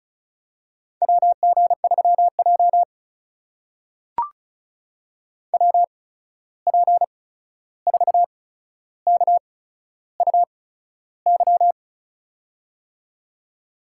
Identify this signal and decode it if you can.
Morse code, heard as a keyed tone, about 36 wpm, spaced as for 13 wpm, 700 Hz. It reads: WG3J WP4KUY
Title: USA Call Signs - 8x Character Spacing - 35wpm